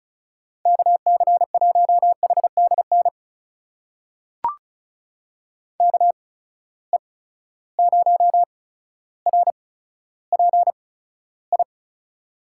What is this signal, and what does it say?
Morse code, heard as a keyed tone, about 36 wpm, spaced as for 15 wpm, 700 Hz. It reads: KC1HDN KE0RPI
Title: USA Call Signs - 8x Character Spacing - 35wpm